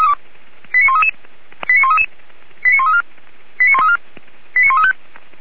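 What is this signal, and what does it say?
Five-tone selective call (ZVEI), heard as quick runs of five tones, tones of about 80 ms. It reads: 89133 89133 89134 89134 89135
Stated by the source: Five-tone ZVEI selective call received by SDR.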